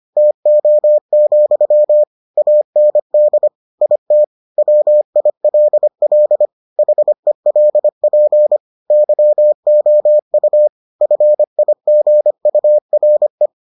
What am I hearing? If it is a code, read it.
Morse code, heard as a keyed tone, about 25 wpm, 600 Hz. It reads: TO, AND IT WILL HELP YOU FIGURE